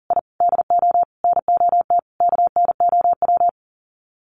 Morse code, heard as a keyed tone, about 40 wpm, 700 Hz. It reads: I DO NOT KNOW
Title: Sets of 5 Words Encore2 40wpm